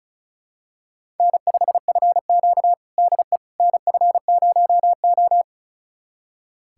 Morse code, heard as a keyed tone, about 36 wpm, 700 Hz. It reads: N5FQ DE NF0O
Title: USA Callsign DE Callsign - 4 Characters - 35wpm